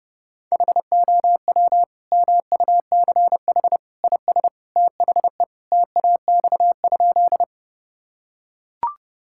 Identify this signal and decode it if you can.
Morse code, heard as a keyed tone, about 30 wpm, 700 Hz. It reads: HOW MUCH IS THE TAX?
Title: Sentences from Top 1000 Words - Review Part 5 - 30wpm